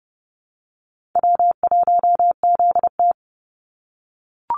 Morse code, heard as a keyed tone, about 30 wpm, 700 Hz. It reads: W1ZT
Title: Easy Call Signs - 30wpm